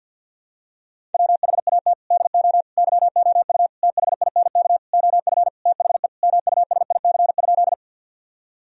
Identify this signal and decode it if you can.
Morse code, heard as a keyed tone, about 52 wpm, 700 Hz. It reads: WHAT DO YOU THINK OF THE MUSIC?
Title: Sentences from Top 1000 Words - Review Part 1 - 50wpm